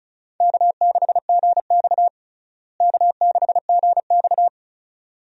Morse code, heard as a keyed tone, about 36 wpm, 700 Hz. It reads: K6GX K6GX
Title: USA Call Signs - 4 Characters - Repeated 2x - 35wpm